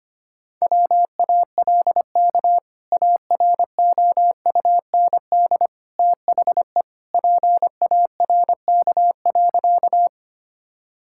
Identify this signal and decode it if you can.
Morse code, heard as a keyed tone, about 25 wpm, 700 Hz. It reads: WALK AROUND THE PARK.
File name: practice / 25wpm / Sentences with 2-Word Phrases - 25wpm